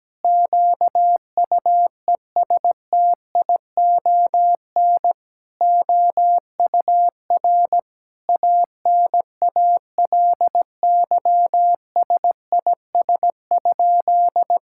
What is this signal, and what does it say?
Morse code, heard as a keyed tone, about 17 wpm, 700 Hz. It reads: QUESTION OUR ANALYSIS?
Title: Sentences from Top 1000 Words - Review Part 5 - 17wpm